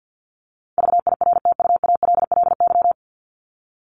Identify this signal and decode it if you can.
Morse code, heard as a keyed tone, about 50 wpm, 700 Hz. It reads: VIRTUALLY
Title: LY Suffix - Top 100 Words with Base Words 50wpm